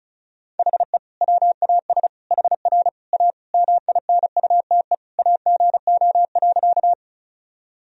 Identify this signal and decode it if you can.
Morse code, heard as a keyed tone, about 36 wpm, 700 Hz. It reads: HE WAS HR A MINUTE AGO.
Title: Sentences from Top 500 Words and Abbreviations 35wpm